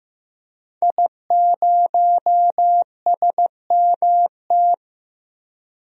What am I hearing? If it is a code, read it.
Morse code, heard as a keyed tone, about 15 wpm, 700 Hz. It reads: I0SMT
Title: DXCC-100 Call Signs - 5 Characters - 15wpm